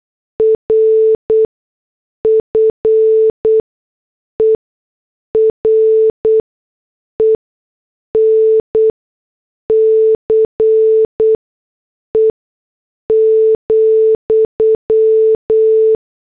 Morse code, heard as a keyed tone, about 8 wpm, spaced as for 6 wpm, 440 Hz. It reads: RFERENCE,